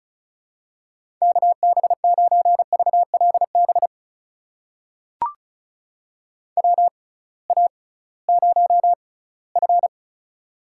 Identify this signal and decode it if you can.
Morse code, heard as a keyed tone, about 36 wpm, spaced as for 20 wpm, 700 Hz. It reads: KB9VLB WA0F
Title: USA Call Signs - 6x Character Spacing - 35wpm